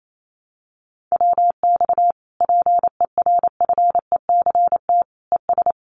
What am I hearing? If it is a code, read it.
Morse code, heard as a keyed tone, about 28 wpm, 700 Hz. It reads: WX PERFECT ES
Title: Common QSO Phrases Encore 28wpm